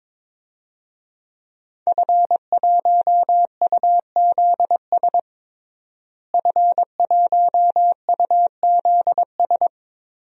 Morse code, heard as a keyed tone, about 22 wpm, 700 Hz. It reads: F1UZS F1UZS
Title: DXCC-100 Call Signs - 5 Characters - Repeated 3x - 22wpm